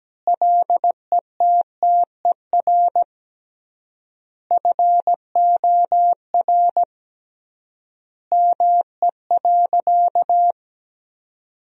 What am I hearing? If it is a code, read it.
Morse code, heard as a keyed tone, about 17 wpm, 700 Hz. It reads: LETTER FOR ME.
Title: 3x Word-Spacing - Sentences from Top 300 Words - 17wpm